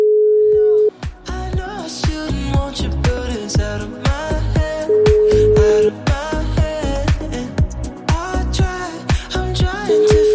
Ringback tone (Europe, 425 Hz), with noise.